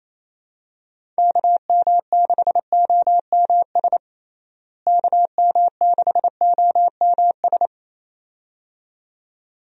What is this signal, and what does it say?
Morse code, heard as a keyed tone, about 27 wpm, 700 Hz. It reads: KM6OMS KM6OMS
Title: USA Call Signs - 6 Characters - Repeated 2x - 28wpm